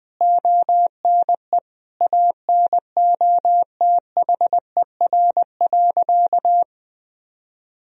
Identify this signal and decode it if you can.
Morse code, heard as a keyed tone, about 20 wpm, 700 Hz. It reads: ONE ANOTHER.
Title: Sentences from Top 1000 Words - Review Part 1 - 20wpm